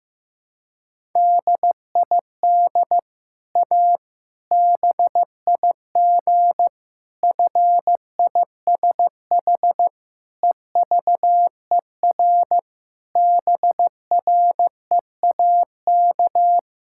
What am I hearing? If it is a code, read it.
Morse code, heard as a keyed tone, about 15 wpm, 700 Hz. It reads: DID A BIG FISH EVER BREAK
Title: Sentences from Top 1000 Words - Review Part 6 - 15wpm